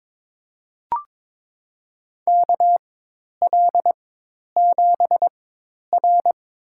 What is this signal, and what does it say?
Morse code, heard as a keyed tone, about 22 wpm, spaced as for 10 wpm, 700 Hz. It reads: KL7R
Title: USA Call Signs - 4x Character Spacing - 22wpm